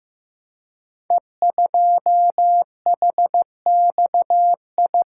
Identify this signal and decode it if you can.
Morse code, heard as a keyed tone, about 15 wpm, 700 Hz. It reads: E2HXI
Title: Code Groups - 5 Letter:Numbers - 15wpm